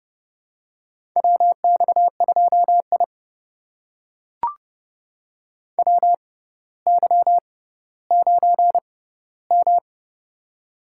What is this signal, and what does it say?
Morse code, heard as a keyed tone, about 32 wpm, spaced as for 16 wpm, 700 Hz. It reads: WX2I WY9M
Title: USA Call Signs - 4 Characters - 6x Character Spacing - 30wpm